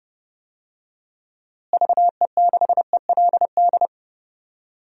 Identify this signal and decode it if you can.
Morse code, heard as a keyed tone, about 30 wpm, 700 Hz. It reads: VE6ELD
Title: DXCC-100 Call Signs - 6 Characters - Repeated 3x - 30wpm